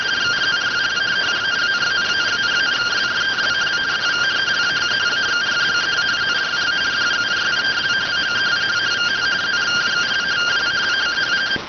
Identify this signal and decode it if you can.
Two-tone teleprinter signal, about 170 Hz shift.